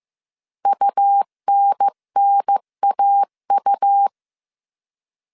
Morse code, heard as a keyed tone, about 15 wpm, 800 Hz. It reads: UNNAU
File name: CW / DET / 1 3 1500C S A 14wpm 2447